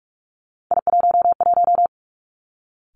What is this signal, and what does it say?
Morse code, heard as a keyed tone, about 46 wpm, 700 Hz. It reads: I11
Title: Three Challenging Letter-Number Combinations 45wpm